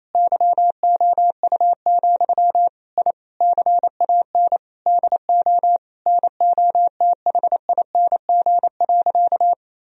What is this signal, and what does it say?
Morse code, heard as a keyed tone, about 28 wpm, 700 Hz. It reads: YOU, I CAN DO NOTHING.
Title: Sentences from Top 1000 Words - Review Part 3 - 28wpm